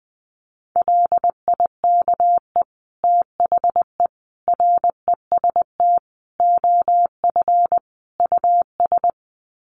Morse code, heard as a keyed tone, about 20 wpm, 700 Hz. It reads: LIKE THE REST OF US
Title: Sets of 5 Words Encore2 20wpm